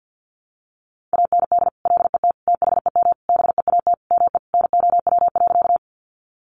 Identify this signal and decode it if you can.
Morse code, heard as a keyed tone, about 50 wpm, 700 Hz. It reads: AND LET THEM BEAT ME NOW.
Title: Sentences from Top 1300 Words 50wpm